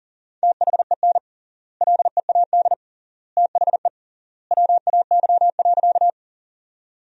Morse code, heard as a keyed tone, about 40 wpm, 700 Hz. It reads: THEN LEAD THE WAY.
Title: 3x Word-Spacing - Sentences from Top 500 Words - 40wpm